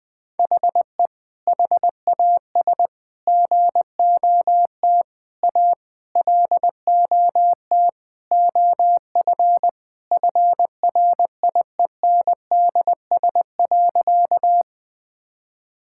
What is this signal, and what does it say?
Morse code, heard as a keyed tone, about 20 wpm, 700 Hz. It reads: HE HAS GOT A LOT OF FRIENDS.
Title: Sentences with 4-Word Phrases - Spoken Beginning - 20wpm